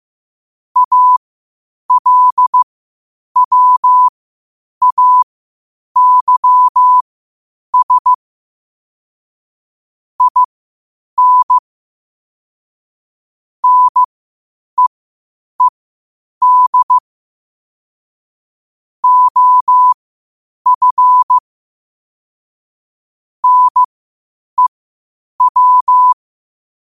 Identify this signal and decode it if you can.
Morse code, heard as a keyed tone, about 15 wpm, spaced as for 8 wpm, 1000 Hz. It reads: ALWAYS IN NEED OF NEW
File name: qotdfiles / 8wpm